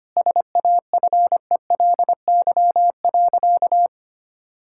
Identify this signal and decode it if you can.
Morse code, heard as a keyed tone, about 25 wpm, 690 Hz. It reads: SAFELY.
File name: Easy Sentences with Hints - 25wpm